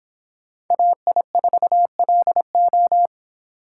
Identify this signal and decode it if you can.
Morse code, heard as a keyed tone, about 29 wpm, 700 Hz. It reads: AI4LO